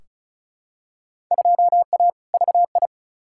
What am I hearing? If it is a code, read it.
Morse code, heard as a keyed tone, about 36 wpm, 700 Hz. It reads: 2A VI
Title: Field Day Exchanges Encore - 35wpm